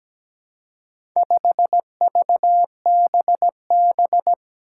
Morse code, heard as a keyed tone, about 17 wpm, 700 Hz. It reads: 5VBB
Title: Four Challenging Letter-Number Combinations 17wpm